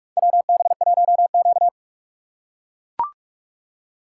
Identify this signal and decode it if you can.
Morse code, heard as a keyed tone, about 43 wpm, 700 Hz. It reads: WB1Q
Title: USA Call Signs - 4 Characters - 3x Character Spacing - 45wpm